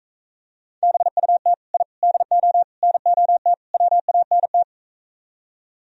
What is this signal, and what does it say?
Morse code, heard as a keyed tone, about 45 wpm, 700 Hz. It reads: BUT I DO NOT WANT